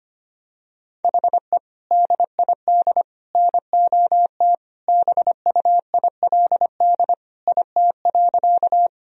Morse code, heard as a keyed tone, about 25 wpm, 700 Hz. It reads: HE DID NOT BUILD IT.
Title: Sentences from Top 1000 Words - Review Part 1 - 25wpm